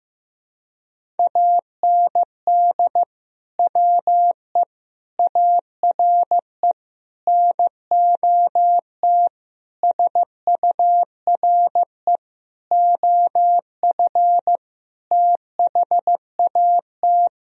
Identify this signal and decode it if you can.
Morse code, heard as a keyed tone, about 15 wpm, 700 Hz. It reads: AND WE ARE NOT SURE OF THAT
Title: Sentences from Top 1000 Words - Review Part 1 - 15wpm